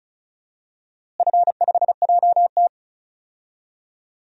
Morse code, heard as a keyed tone, about 34 wpm, 700 Hz. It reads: F5JT
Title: DXCC-100 Call Signs - 4 Characters - Repeated 3x - 35wpm